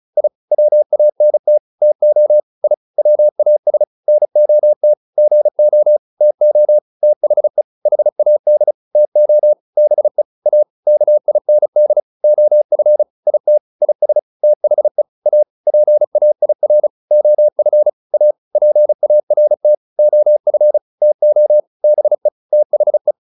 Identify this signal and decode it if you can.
Morse code, heard as a keyed tone, about 35 wpm, 600 Hz. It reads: I WANT TO I WAS NOT GO TO THE HAD TO BE A KIND OF IT IS THE A PAIR OF A PART OF TO BE THE